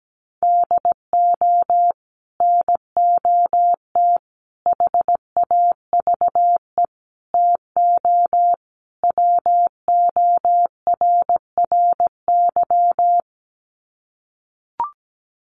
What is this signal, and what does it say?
Morse code, heard as a keyed tone, about 17 wpm, 700 Hz. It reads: DO NOT HAVE TO WORRY
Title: Sets of 5 Words 17wpm